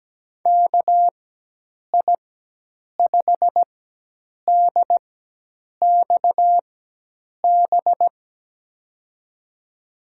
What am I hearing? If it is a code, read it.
Morse code, heard as a keyed tone, about 17 wpm, spaced as for 8 wpm, 700 Hz. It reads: KI5DXB
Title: USA Call Signs - 6 Characters - 4x Character Spacing - 17wpm